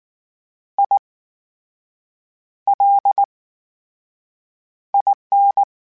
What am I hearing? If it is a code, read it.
Morse code, heard as a keyed tone, about 19 wpm, spaced as for 6 wpm, 800 Hz. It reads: ILIN
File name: Special INGs / Special INGs-mp3s / VerySlowToFaster Ing @18